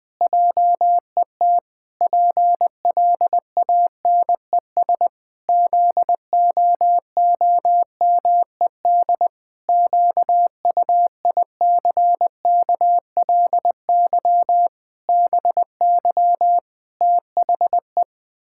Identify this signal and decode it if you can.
Morse code, heard as a keyed tone, about 20 wpm, 700 Hz. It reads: JET PLANES ZOOMED QUICKLY BY THE
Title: Pangram Challenge v2 - 20wpm